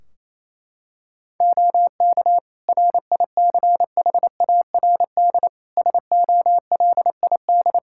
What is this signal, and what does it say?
Morse code, heard as a keyed tone, about 28 wpm, 700 Hz. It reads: OK RICHARD SOLID
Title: Practice QSO - Exchange 3 - More Info - 28wpm